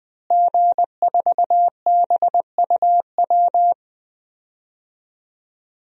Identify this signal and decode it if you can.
Morse code, heard as a keyed tone, about 20 wpm, 700 Hz. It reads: G4BUW
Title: DXCC-100 Call Signs - 5 Characters - Repeated 3x - 20wpm